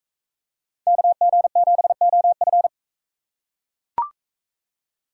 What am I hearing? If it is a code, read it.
Morse code, heard as a keyed tone, about 39 wpm, 700 Hz. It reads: KG7OF